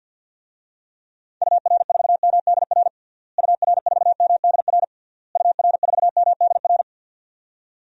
Morse code, heard as a keyed tone, about 47 wpm, 700 Hz. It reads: UR4MDR UR4MDR UR4MDR
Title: DXCC-100 Call Signs - 6 Characters - Repeated 3x - 50wpm